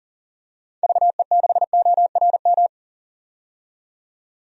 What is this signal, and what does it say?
Morse code, heard as a keyed tone, about 40 wpm, 700 Hz. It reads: VE6ORM
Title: DXCC-100 Call Signs - 6 Characters - 40wpm